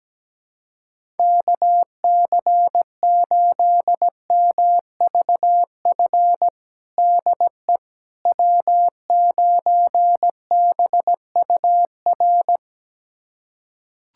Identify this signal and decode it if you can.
Morse code, heard as a keyed tone, about 17 wpm, 700 Hz. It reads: KC8MVF DE W9BUR
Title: USA Callsign DE Callsign 17wpm